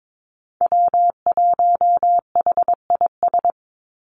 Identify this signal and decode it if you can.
Morse code, heard as a keyed tone, about 22 wpm, 700 Hz. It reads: W1HIS
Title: Call Signs - 22wpm